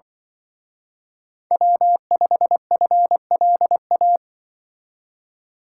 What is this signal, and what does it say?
Morse code, heard as a keyed tone, about 24 wpm, 700 Hz. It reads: W5FLA
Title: Speed Racing - USA 1x3 Call Signs 20wpm